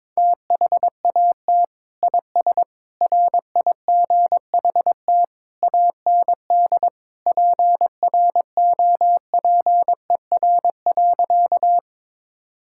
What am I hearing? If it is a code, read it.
Morse code, heard as a keyed tone, about 22 wpm, 700 Hz. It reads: THAT IS RIGHT AND PROPER.
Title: Sentences from Top 1000 Words - Review Part 5 - 22wpm